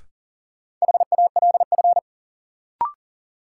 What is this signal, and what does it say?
Morse code, heard as a keyed tone, about 40 wpm, 700 Hz. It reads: HALF
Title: Sets of 3 Words - New Word Drill - 40wpm